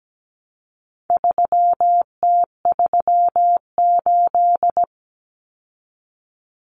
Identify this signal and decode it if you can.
Morse code, heard as a keyed tone, about 17 wpm, 700 Hz. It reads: 3T38
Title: Common Cut Numbers - 4-Digit 17wpm